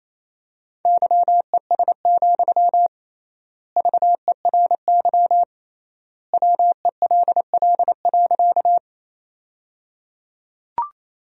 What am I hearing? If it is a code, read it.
Morse code, heard as a keyed tone, about 28 wpm, 700 Hz. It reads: YES, VERY WELL.
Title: 3x Word-Spacing - Sentences from Top 500 Words - 28wpm